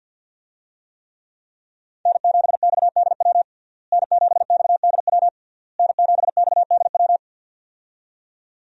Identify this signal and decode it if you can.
Morse code, heard as a keyed tone, about 51 wpm, 700 Hz. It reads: N7XDW N7XDW N7XDW
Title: USA Call Signs - 5 Characters - Repeated 3x - 50wpm